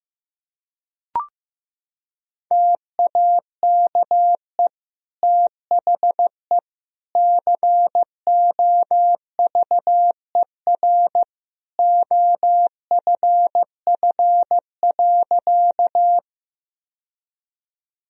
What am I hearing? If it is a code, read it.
Morse code, heard as a keyed tone, about 15 wpm, 700 Hz. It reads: TAKE THE COVER OFF.
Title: Sentences from Top 1000 Words - Review Part 1 - 15wpm